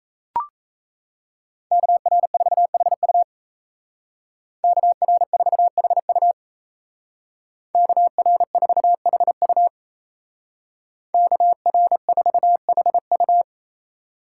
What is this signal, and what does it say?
Morse code, heard as a keyed tone, about 33 wpm, 700 Hz. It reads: KR4HU KR4HU KR4HU KR4HU
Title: Speed Racing - USA 2x2 Call Signs 28wpm